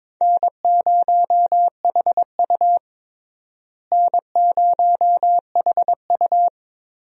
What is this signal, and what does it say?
Morse code, heard as a keyed tone, about 21 wpm, 700 Hz. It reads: N0HU N0HU
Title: USA Call Signs - 4 Characters - Repeated 2x - 22wpm